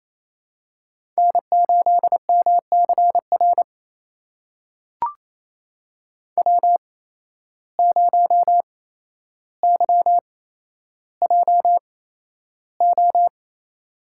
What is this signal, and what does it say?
Morse code, heard as a keyed tone, about 28 wpm, spaced as for 12 wpm, 700 Hz. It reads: N8MCR W0YJO
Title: USA Call Signs - 5 Characters - 8x Character Spacing - 28wpm